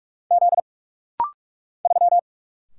Morse code, heard as a keyed tone, about 44 wpm, 700 Hz. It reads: Z 3
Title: Single Letter-Number - Rapid-Fire 45wpm